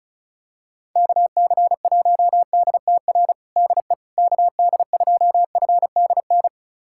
Morse code, heard as a keyed tone, about 36 wpm, 700 Hz. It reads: KC1DTR DE KD2FDN
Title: USA Callsign DE Callsign 35wpm